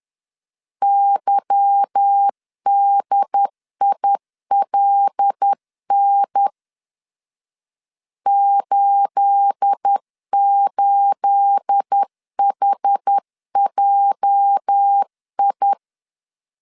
Morse code, heard as a keyed tone, about 10 wpm, 800 Hz. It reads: YDILN 88HJI
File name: LEZIONI / 9 28 10wpm 2842